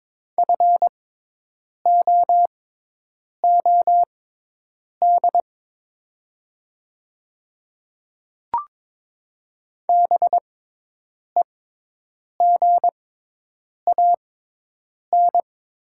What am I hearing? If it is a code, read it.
Morse code, heard as a keyed tone, about 22 wpm, spaced as for 8 wpm, 700 Hz. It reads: FOOD BEGAN
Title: 6x Character Spacing - Top 300 Words - 22wpm